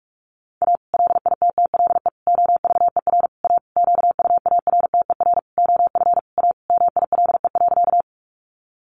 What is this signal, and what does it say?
Morse code, heard as a keyed tone, about 44 wpm, 700 Hz. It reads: A LITTLE OVER A QUARTER OF A MILE.
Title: Sentences from Top 1300 Words 45wpm